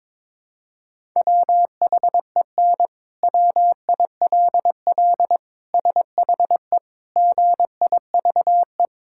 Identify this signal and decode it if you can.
Morse code, heard as a keyed tone, about 21 wpm, 700 Hz. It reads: WHEN WILL SHE GIVE
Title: Sentences from Top 1000 Words - Review Part 5 - 22wpm